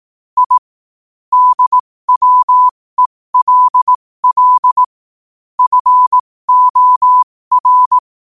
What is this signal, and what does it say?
Morse code, heard as a keyed tone, about 18 wpm, spaced as for 15 wpm, 1000 Hz. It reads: I DWELL FOR
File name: qotdfiles / 15wpm